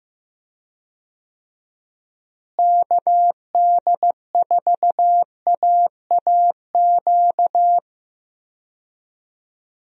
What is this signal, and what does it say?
Morse code, heard as a keyed tone, about 16 wpm, 700 Hz. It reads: KD4AAQ
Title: USA Call Signs - 6 Characters - Repeated 3x - 15wpm